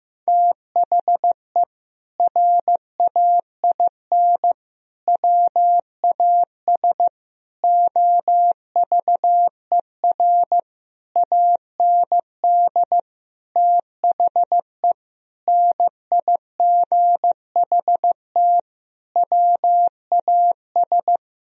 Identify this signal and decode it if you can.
Morse code, heard as a keyed tone, about 16 wpm, 700 Hz. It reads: THE RAIN WAS OVER AND THE NIGHT WAS
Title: Sentences from Top 1000 Words - Review Part 1 - 15wpm